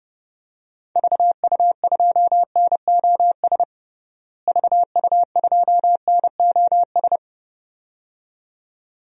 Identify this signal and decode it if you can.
Morse code, heard as a keyed tone, about 30 wpm, 700 Hz. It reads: VU2NOS VU2NOS
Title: DXCC-100 Call Signs - 6 Characters - Repeated 2x - 30wpm